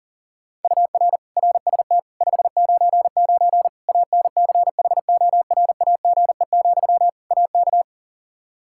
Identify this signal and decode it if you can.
Morse code, heard as a keyed tone, about 40 wpm, 700 Hz. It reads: UR RST 599 ANCHORAGE, AK
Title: K1USN SST QSO Practice 40wpm